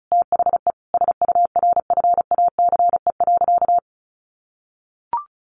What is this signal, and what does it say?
Morse code, heard as a keyed tone, about 34 wpm, 700 Hz. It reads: THE SURFACE.
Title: Sentences from Top 600 Words 35wpm